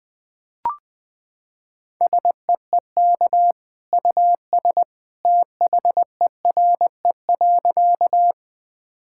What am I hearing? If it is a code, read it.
Morse code, heard as a keyed tone, about 21 wpm, 700 Hz. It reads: SEEK US THERE.
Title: Sentences from Top 1600 Words 20wpm